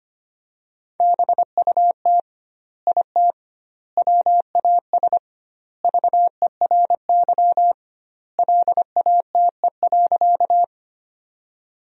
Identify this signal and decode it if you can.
Morse code, heard as a keyed tone, about 25 wpm, 700 Hz. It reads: BUT IT WAS VERY LATE.
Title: 2x Word-Spacing - Sentences from Top 300 Words - 25wpm